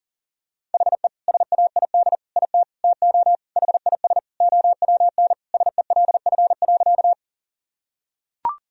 Morse code, heard as a keyed tone, about 40 wpm, 700 Hz. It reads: HE SAID IT TO HIS OWN SELF.
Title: Sentences from Top 1000 Words - Review Part 1 - 40wpm